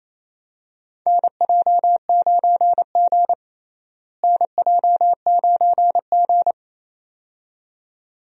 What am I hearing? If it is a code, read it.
Morse code, heard as a keyed tone, about 28 wpm, 700 Hz. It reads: NJ9G NJ9G
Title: USA Call Signs - 4 Characters - Repeated 2x - 28wpm